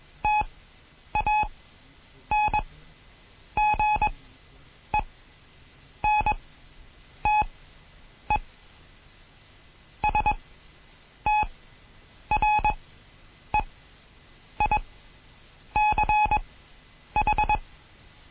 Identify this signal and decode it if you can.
Morse code, heard as a keyed tone, about 21 wpm, spaced as for 8 wpm, 850 Hz. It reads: TANGENTE STREICH
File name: Morsen / Kurs A 20